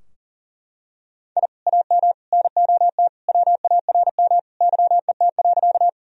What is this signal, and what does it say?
Morse code, heard as a keyed tone, about 40 wpm, 700 Hz. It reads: I AM NOT WARM YET.
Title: Sentences from Top 1000 Words - Review Part 2 - 40wpm